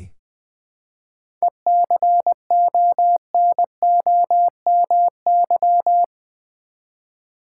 Morse code, heard as a keyed tone, about 20 wpm, 690 Hz. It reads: ECONOMY
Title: 7 Letter Words - First 3 Letters Given - 20wpm